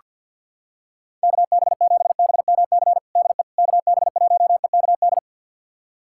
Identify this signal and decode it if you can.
Morse code, heard as a keyed tone, about 48 wpm, 700 Hz. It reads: KB7BMC DE KB1EKD
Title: USA Callsign DE Callsign - 6 Characters - 50wpm